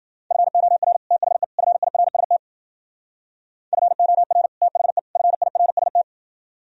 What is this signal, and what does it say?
Morse code, heard as a keyed tone, about 51 wpm, 700 Hz. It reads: FOR THE FIRST FOR THE FIRST